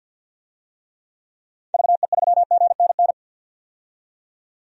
Morse code, heard as a keyed tone, about 51 wpm, 700 Hz. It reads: VE3GNN
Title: DXCC-100 Call Signs - 6 Characters - 50wpm